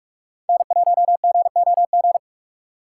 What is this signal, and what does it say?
Morse code, heard as a keyed tone, about 43 wpm, 700 Hz. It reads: N1GOG
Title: USA Call Signs - 5 Characters - 6x Character Spacing - 45wpm